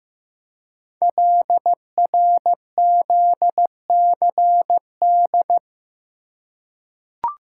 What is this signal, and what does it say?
Morse code, heard as a keyed tone, about 15 wpm, 700 Hz. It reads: LRZCD
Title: Code Groups - 5 Letters - 15wpm